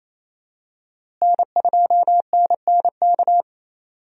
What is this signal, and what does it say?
Morse code, heard as a keyed tone, about 28 wpm, 700 Hz. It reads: N2NNK
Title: USA Call Signs - 5 Characters - Repeated 2x - 28wpm